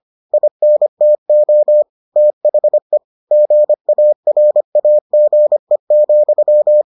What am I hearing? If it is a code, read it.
Morse code, heard as a keyed tone, about 25 wpm, 600 Hz. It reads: INTO THE GARAGE,